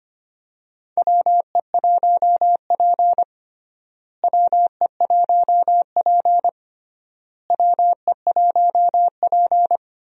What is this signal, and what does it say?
Morse code, heard as a keyed tone, about 25 wpm, 700 Hz. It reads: WE1P WE1P WE1P
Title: USA Call Signs - 4 Characters - Repeated 3x - 25wpm